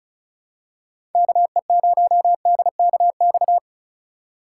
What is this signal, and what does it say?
Morse code, heard as a keyed tone, about 34 wpm, 700 Hz. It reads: KE0DKX
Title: USA Call Signs - 6 Characters - 4x Character Spacing - 35wpm